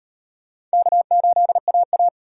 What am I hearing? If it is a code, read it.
Morse code, heard as a keyed tone, about 36 wpm, 700 Hz. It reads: K8AA